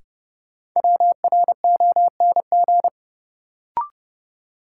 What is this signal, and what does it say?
Morse code, heard as a keyed tone, about 30 wpm, 700 Hz. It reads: WRONG
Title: Sets of 5 Words Encore - New Word Drill - 30wpm